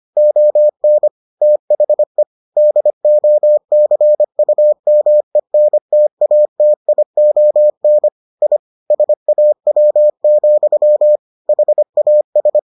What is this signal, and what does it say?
Morse code, heard as a keyed tone, about 25 wpm, 600 Hz. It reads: ON THE DOCUMENTATION I SAW, HAS